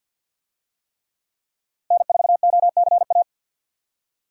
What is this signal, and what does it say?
Morse code, heard as a keyed tone, about 46 wpm, 700 Hz. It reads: N4OCA
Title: USA Call Signs - 5 Characters - Repeated 3x - 50wpm